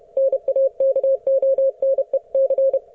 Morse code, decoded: NAKONEC